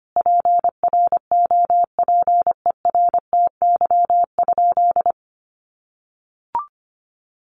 Morse code, heard as a keyed tone, about 25 wpm, 700 Hz. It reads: PROPERTY?
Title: Sentences from Top 1500 Words 25wpm